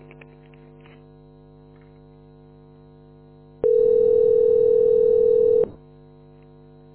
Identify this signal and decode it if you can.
Ringback tone (North America), 440 + 480 Hz.
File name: f80dc860-67e6-4459-8175